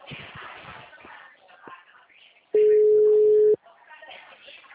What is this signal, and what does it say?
Ringback tone (Europe, 425 Hz), with speech.
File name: out- -2009- - - .39899